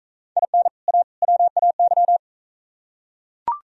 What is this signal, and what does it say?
Morse code, heard as a keyed tone, about 41 wpm, 700 Hz. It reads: IN A WAY